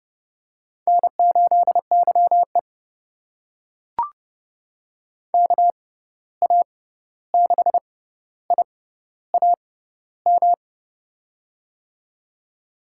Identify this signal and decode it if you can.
Morse code, heard as a keyed tone, about 32 wpm, spaced as for 14 wpm, 700 Hz. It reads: N8YE KA6IAM
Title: USA Call Signs - 6x Character Spacing - 30wpm